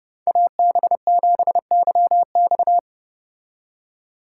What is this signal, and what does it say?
Morse code, heard as a keyed tone, about 29 wpm, 700 Hz. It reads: AB7YX